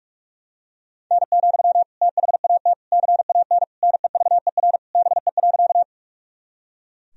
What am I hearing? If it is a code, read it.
Morse code, heard as a keyed tone, about 42 wpm, 700 Hz. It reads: N, THAT CAN NEVER BE.
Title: Sentences from Top 300 Words and Abbreviations 45wpm